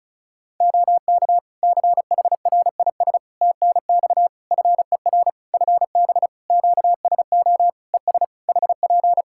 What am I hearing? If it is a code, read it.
Morse code, heard as a keyed tone, about 34 wpm, 700 Hz. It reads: OK CHRIS TNX FER FB QSO ES HP
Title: Practice QSO - Exchange 4 - Ending - 35wpm